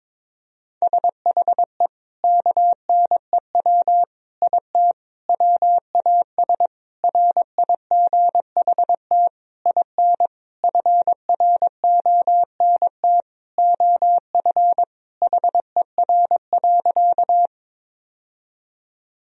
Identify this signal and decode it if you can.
Morse code, heard as a keyed tone, about 22 wpm, 700 Hz. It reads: SHE KNEW IT WAS RIGHT IN FRONT OF HER.
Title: Sentences from Top 1000 Words - Review Part 2 - 22wpm